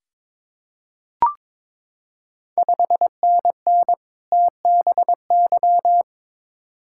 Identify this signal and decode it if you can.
Morse code, heard as a keyed tone, about 22 wpm, 700 Hz. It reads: 5NN TBY
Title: Ontario QSO Party Exchange 22wpm